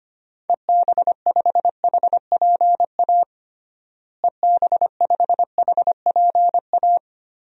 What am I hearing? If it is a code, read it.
Morse code, heard as a keyed tone, about 25 wpm, 700 Hz. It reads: EB5HPA EB5HPA
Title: DXCC-100 Call Signs - 6 Characters - Repeated 3x - 25wpm